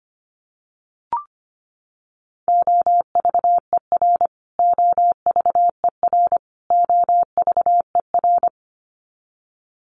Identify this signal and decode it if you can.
Morse code, heard as a keyed tone, about 25 wpm, 700 Hz. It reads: OVER OVER OVER
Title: Top 500 Words Repeated 25wpm